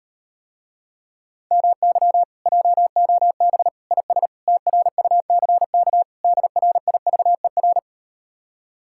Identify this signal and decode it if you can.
Morse code, heard as a keyed tone, about 38 wpm, 700 Hz. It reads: MY JOB IS TRUCK DRIVER